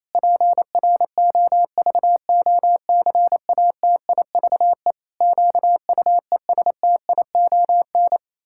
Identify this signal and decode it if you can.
Morse code, heard as a keyed tone, about 28 wpm, 700 Hz. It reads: PROVOCATIVE QUESTION
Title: Pangram Challenge v2 - 28wpm